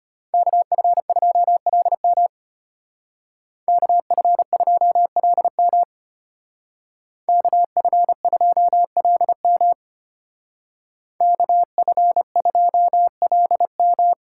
Morse code, heard as a keyed tone, about 33 wpm, 700 Hz. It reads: KF2LM KF2LM KF2LM KF2LM